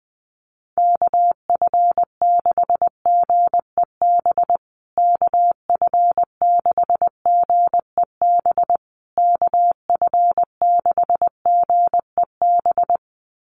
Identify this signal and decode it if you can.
Morse code, heard as a keyed tone, about 20 wpm, 700 Hz. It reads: KF6GEB KF6GEB KF6GEB
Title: USA Call Signs Repeated 3x 20wpm